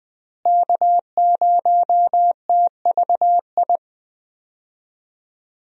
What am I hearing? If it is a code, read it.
Morse code, heard as a keyed tone, about 20 wpm, 700 Hz. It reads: K0TVI
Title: USA Call Signs - 5 Characters - Repeated 3x - 20wpm